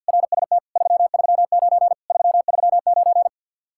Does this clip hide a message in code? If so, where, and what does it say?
Yes: Morse code from 0.1 to 3.3 s, heard as a keyed tone, about 64 wpm, 700 Hz. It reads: RST 339 339